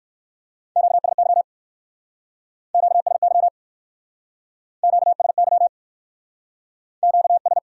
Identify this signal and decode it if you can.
Morse code, heard as a keyed tone, about 59 wpm, 700 Hz. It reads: QSY QSY QSY Q S